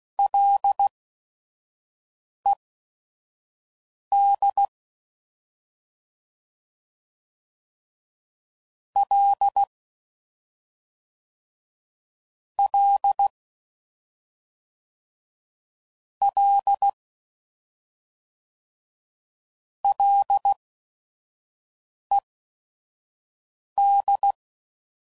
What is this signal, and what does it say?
Morse code, heard as a keyed tone, about 16 wpm, spaced as for 5 wpm, 800 Hz. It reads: LED L L L LED
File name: Prefixes / FLs & THs / Single L's